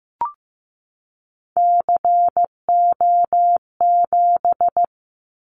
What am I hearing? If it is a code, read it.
Morse code, heard as a keyed tone, about 15 wpm, 700 Hz. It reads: CO7
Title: Callsign Prefixes for Top 100 DXCC - 15wpm